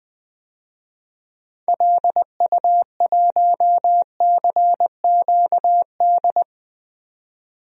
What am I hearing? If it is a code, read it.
Morse code, heard as a keyed tone, about 20 wpm, 700 Hz. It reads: LU1CQD
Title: DXCC-100 Call Signs - 6 Characters - 20wpm